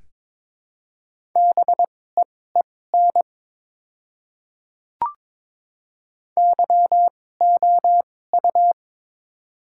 Morse code, heard as a keyed tone, about 22 wpm, spaced as for 16 wpm, 700 Hz. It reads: BEEN YOU
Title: 2x Character Spacing - Top 200 Words - 22wpm